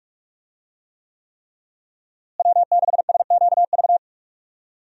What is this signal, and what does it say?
Morse code, heard as a keyed tone, about 46 wpm, 700 Hz. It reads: W6SQV
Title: USA Call Signs - 5 Characters - Repeated 3x - 45wpm